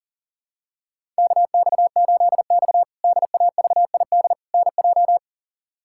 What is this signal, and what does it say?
Morse code, heard as a keyed tone, about 37 wpm, 700 Hz. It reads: KX8X DAVID NJ
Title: K1USN SST QSO Practice - 40wpm